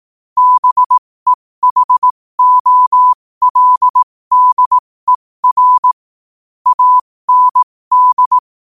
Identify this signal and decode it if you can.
Morse code, heard as a keyed tone, about 18 wpm, spaced as for 15 wpm, 1000 Hz. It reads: BEHOLDER AND
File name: qotdfiles / 15wpm